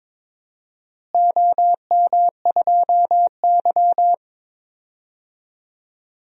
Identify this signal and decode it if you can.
Morse code, heard as a keyed tone, about 22 wpm, 700 Hz. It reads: OM2Y
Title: DXCC-100 Call Signs - 4 Characters - Repeated 2x - 22wpm